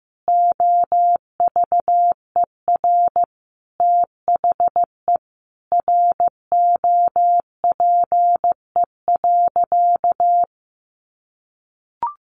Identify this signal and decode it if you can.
Morse code, heard as a keyed tone, about 15 wpm, 700 Hz. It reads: OVER THE ROPE.
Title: Easy Phrases 15wpm